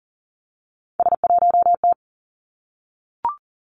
Morse code, heard as a keyed tone, about 40 wpm, 700 Hz. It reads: S1T
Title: Three Challenging Letter-Number Combinations 40wpm